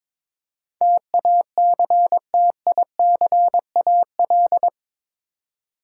Morse code, heard as a keyed tone, about 22 wpm, 700 Hz. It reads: TACTICAL
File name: practice / 22wpm / Eight Letter Words - 22wpm